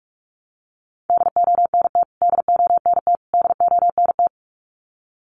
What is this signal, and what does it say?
Morse code, heard as a keyed tone, about 44 wpm, 700 Hz. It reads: DONT DONT DONT
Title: Top 500 Words Repeated 45wpm